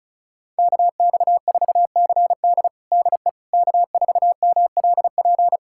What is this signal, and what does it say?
Morse code, heard as a keyed tone, about 34 wpm, 700 Hz. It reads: KX4CD DE K4MLP
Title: USA Callsign DE Callsign - 5 Characters - 35wpm